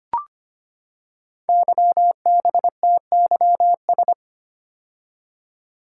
Morse code, heard as a keyed tone, about 25 wpm, 700 Hz. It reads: YBTYS
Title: Code Groups - 5 Letters - 25wpm